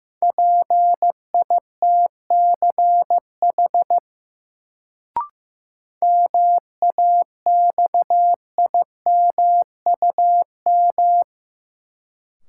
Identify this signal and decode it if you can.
Morse code, heard as a keyed tone, about 15 wpm, 700 Hz. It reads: PITCH MAXIMUM
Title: Sentences from Top 1300 Words - New Word Drill - 15wpm